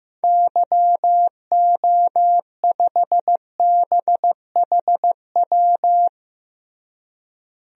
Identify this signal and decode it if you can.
Morse code, heard as a keyed tone, about 15 wpm, 700 Hz. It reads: YO5BHW
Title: DXCC-100 Call Signs - 6 Characters - Repeated 2x - 15wpm